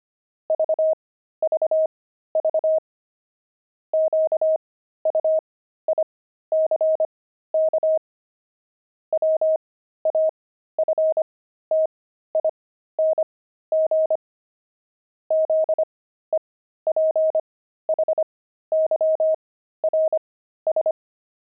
Morse code, heard as a keyed tone, about 25 wpm, spaced as for 13 wpm, 630 Hz. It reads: VVV QUICK WAFTING ZEPHYRS
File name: pangram-fs-13-cs-25-zephyrs